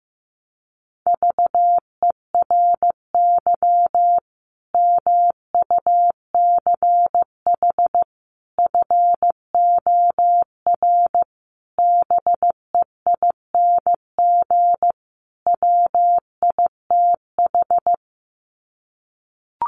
Morse code, heard as a keyed tone, about 15 wpm, 700 Hz. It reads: VERY MUCH FOR BEING WITH
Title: Sets of 5 Words 15wpm